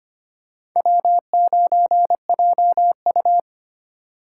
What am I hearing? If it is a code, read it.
Morse code, heard as a keyed tone, about 25 wpm, 700 Hz. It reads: W9JU
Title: USA Call Signs - 4 Characters - 3x Character Spacing - 25wpm